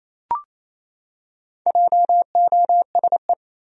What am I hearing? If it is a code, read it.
Morse code, heard as a keyed tone, about 28 wpm, 700 Hz. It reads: JOSE
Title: USA Common Names - Max 5 Letters - 28wpm